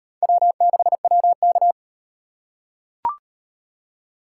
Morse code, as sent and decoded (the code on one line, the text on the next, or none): .-- -.... .-- -.-
W6WK